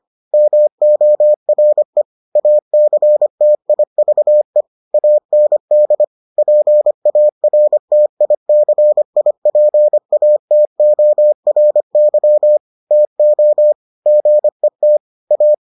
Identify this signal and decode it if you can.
Morse code, heard as a keyed tone, about 25 wpm, 600 Hz. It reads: MORE ACTIVE AND PARTICIPATORY TO GET A